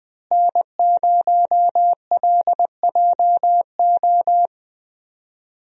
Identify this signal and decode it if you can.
Morse code, heard as a keyed tone, about 21 wpm, 700 Hz. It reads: N0LJO
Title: USA Call Signs - 5 Characters - Repeated 3x - 20wpm